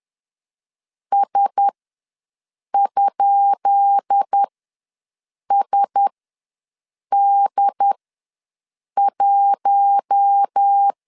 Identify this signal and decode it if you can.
Morse code, heard as a keyed tone, about 10 wpm, spaced as for 6 wpm, 800 Hz. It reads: S?SD1
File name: LEZIONI / 10 SIGNS / 10B INT ws 10wpm 4134